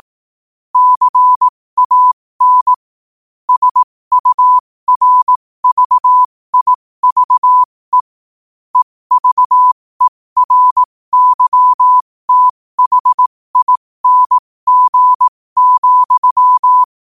Morse code, heard as a keyed tone, about 18 wpm, spaced as for 15 wpm, 1000 Hz. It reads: CAN SURVIVE EVERYTHING,
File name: qotdfiles / 15wpm